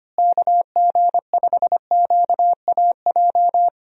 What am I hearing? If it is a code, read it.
Morse code, heard as a keyed tone, about 25 wpm, 700 Hz. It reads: KG5QAJ
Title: USA Call Signs - 6 Characters - Repeated 2x - 25wpm